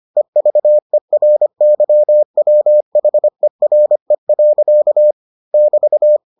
Morse code, heard as a keyed tone, about 25 wpm, 600 Hz. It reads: EVERYWHERE. <BT>